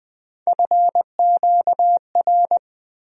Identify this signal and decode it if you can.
Morse code, heard as a keyed tone, about 21 wpm, 700 Hz. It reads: FQR
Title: USA Call Sign Postfixes - 3 Letters 20wpm